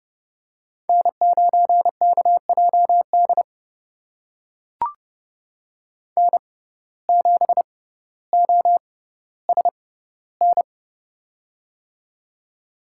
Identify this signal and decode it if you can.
Morse code, heard as a keyed tone, about 30 wpm, spaced as for 15 wpm, 700 Hz. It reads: N9KJD N7OSN
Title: USA Call Signs - 5 Characters - 6x Character Spacing - 30wpm